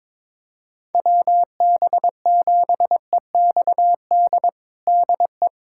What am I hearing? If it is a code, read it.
Morse code, heard as a keyed tone, about 21 wpm, 700 Hz. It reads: WB7EXD DE
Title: USA Callsign DE Callsign 22wpm